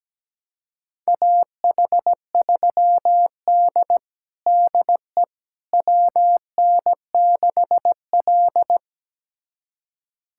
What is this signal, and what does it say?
Morse code, heard as a keyed tone, about 17 wpm, 700 Hz. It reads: AH3D DE WN6L
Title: USA Callsign DE Callsign - 4 Characters - 17wpm